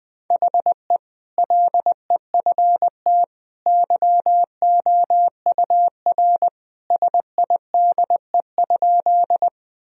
Morse code, heard as a keyed tone, about 20 wpm, 700 Hz. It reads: HE LEFT YOUR SIDE?
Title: Sentences from Top 1000 Words - Review Part 1 - 20wpm